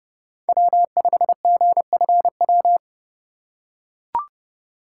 Morse code, heard as a keyed tone, about 30 wpm, 700 Hz. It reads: W5GFW
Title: USA Call Signs - 2x Character Spacing - 30wpm